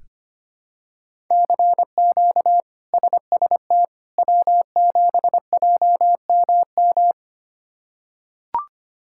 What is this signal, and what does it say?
Morse code, heard as a keyed tone, about 25 wpm, 700 Hz. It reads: CQ SST W7JMM
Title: K1USN SST QSO Practice - 25wpm